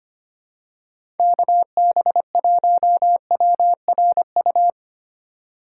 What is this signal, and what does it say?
Morse code, heard as a keyed tone, about 25 wpm, 700 Hz. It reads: KB1WRU
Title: USA Call Signs - 6 Characters - 4x Character Spacing - 25wpm